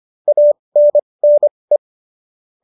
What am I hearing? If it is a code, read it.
Morse code, heard as a keyed tone, about 25 wpm, spaced as for 20 wpm, 570 Hz. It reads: ANNE